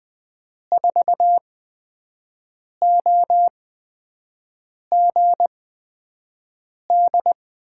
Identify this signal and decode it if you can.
Morse code, heard as a keyed tone, about 20 wpm, spaced as for 5 wpm, 700 Hz. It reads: 4OGD
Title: USA Call Signs - 8x Character Spacing - 20wpm